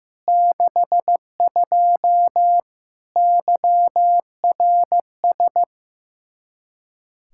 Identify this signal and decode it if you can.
Morse code, heard as a keyed tone, about 15 wpm, 700 Hz. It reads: 62 YRS
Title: Practice QSO - Exchange 3 - More Info - 15wpm